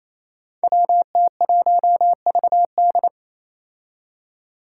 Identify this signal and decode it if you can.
Morse code, heard as a keyed tone, about 28 wpm, 700 Hz. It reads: WT1VD
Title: Five Challenging Letter-Number Combinations 28wpm